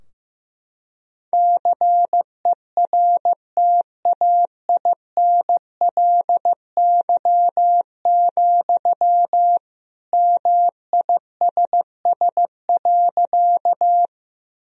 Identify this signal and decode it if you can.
Morse code, heard as a keyed tone, about 15 wpm, 700 Hz. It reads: CERTAINLY, MISS.
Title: Sentences from Top 1000 Words - Review Part 3 - 15wpm